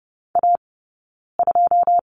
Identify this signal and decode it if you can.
Morse code, heard as a keyed tone, about 30 wpm, 700 Hz. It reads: A 2
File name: Lesson 17d - Letter u - Two Characters 30wpm